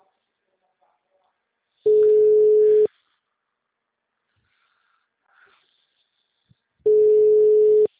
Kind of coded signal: telephone call progress tone

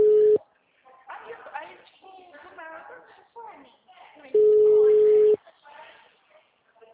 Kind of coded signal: telephone call progress tone

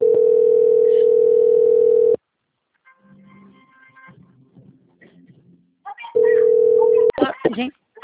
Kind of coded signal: telephone call progress tone

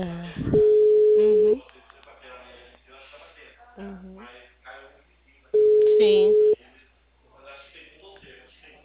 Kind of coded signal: telephone call progress tone

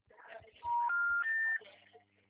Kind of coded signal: telephone call progress tone

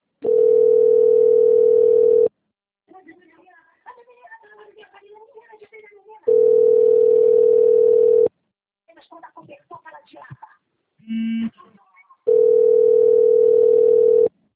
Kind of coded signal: telephone call progress tone

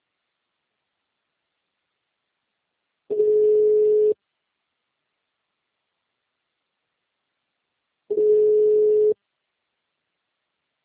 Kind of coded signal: telephone call progress tone

